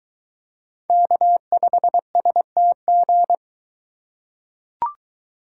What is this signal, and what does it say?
Morse code, heard as a keyed tone, about 23 wpm, 700 Hz. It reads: K5STG